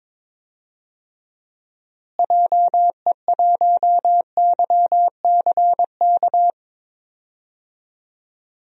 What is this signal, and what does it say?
Morse code, heard as a keyed tone, about 21 wpm, 700 Hz. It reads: JE1YCK
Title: DXCC-100 Call Signs - 6 Characters - Repeated 3x - 22wpm